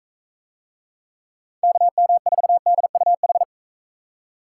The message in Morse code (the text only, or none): KM4DUH